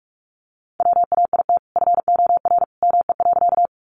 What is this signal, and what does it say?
Morse code, heard as a keyed tone, about 44 wpm, 700 Hz. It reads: WAIT FOR ME.
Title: Sentences from Top 500 Words 45wpm